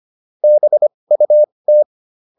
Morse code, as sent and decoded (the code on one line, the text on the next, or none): -... ..- -
BUT